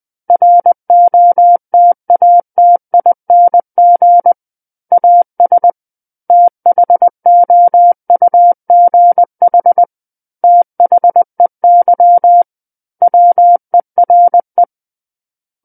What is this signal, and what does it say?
Morse code, heard as a keyed tone, about 20 wpm, 700 Hz. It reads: ROTATING AS THOUGH THEY WERE
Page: SKCC Learning Center - Morse Code Practice Files